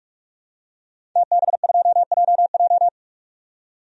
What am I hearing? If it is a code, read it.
Morse code, heard as a keyed tone, about 42 wpm, 700 Hz. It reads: TB2JJ
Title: DXCC-100 Call Signs - 5 Characters - 45wpm